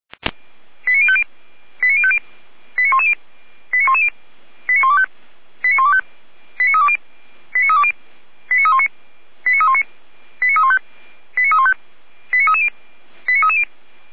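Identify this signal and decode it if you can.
Five-tone selective call (ZVEI), heard as quick runs of five tones, tones of about 70 ms. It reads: 89050 89050 89110 89110 89125 89125 89320 89320 89319 89319 89315 89315 89330 89330
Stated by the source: ZVEI five-tone selective call received with an SDR.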